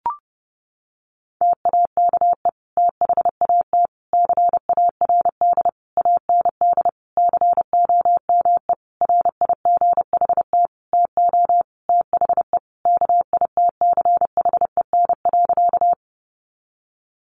Morse code, heard as a keyed tone, about 30 wpm, 700 Hz. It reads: TAKE THAT CARD AND COME RIGHT TO THE KITCHEN.
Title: Sentences from Top 1400 Words 30wpm